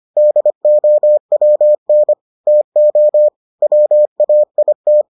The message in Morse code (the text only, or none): DOWN TO WAIT